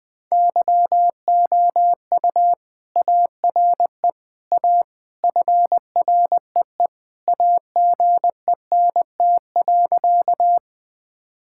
Morse code, heard as a keyed tone, about 20 wpm, 700 Hz. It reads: YOU ARE A FREE AGENT.
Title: Sentences from Top 1600 Words 20wpm